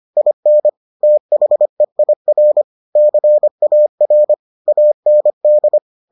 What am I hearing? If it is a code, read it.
Morse code, heard as a keyed tone, about 25 wpm, 600 Hz. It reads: IN THEIR CAR AND